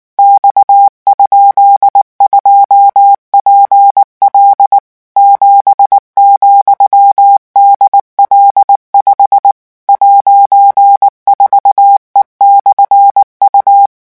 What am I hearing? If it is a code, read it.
Morse code, heard as a keyed tone, about 19 wpm, 800 Hz. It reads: X?2PL 7,DL5 '4E/U
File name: morse mp3 / revision15-vitesse 18 mots